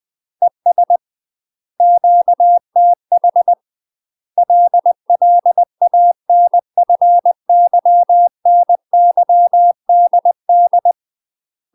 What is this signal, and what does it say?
Morse code, heard as a keyed tone, about 20 wpm, 700 Hz. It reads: ES QTH LLANFYNYDD
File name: UK MASTER QSOs TRAINING / 20WPM / UK MASTER QSOs 0038